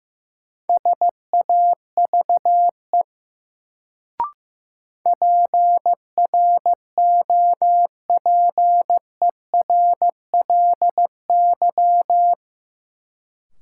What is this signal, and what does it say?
Morse code, heard as a keyed tone, about 15 wpm, 700 Hz. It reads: SAVE PROPERLY
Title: Sentences from Top 1100 Words - New Word Drill - 15wpm